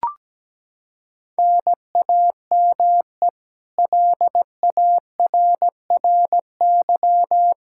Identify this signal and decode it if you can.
Morse code, heard as a keyed tone, about 17 wpm, 700 Hz. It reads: NAME LARRY
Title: Practice QSO - Exchange 1 - Beginning - 17wpm